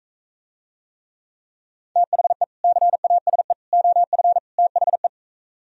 Morse code, heard as a keyed tone, about 44 wpm, 700 Hz. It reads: THE CASE OF THE